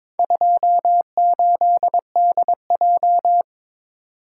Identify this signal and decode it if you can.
Morse code, heard as a keyed tone, about 22 wpm, 700 Hz. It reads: 28DJ
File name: practice / 22wpm / Four Letter-Number Combinations - 22wpm